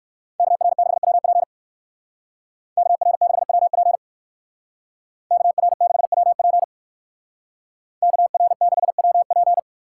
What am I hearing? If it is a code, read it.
Morse code, heard as a keyed tone, about 57 wpm, 700 Hz. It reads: KR6WP KR6WP KR6WP KR6WP